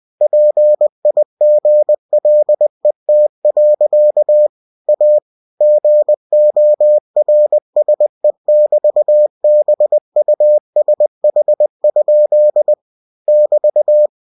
Morse code, heard as a keyed tone, about 20 wpm, 600 Hz. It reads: PIGLET. A GORSE<BT>BUSH? <BT>
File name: cw-books / winnie-the-pooh / winnie-20wpm-page-0082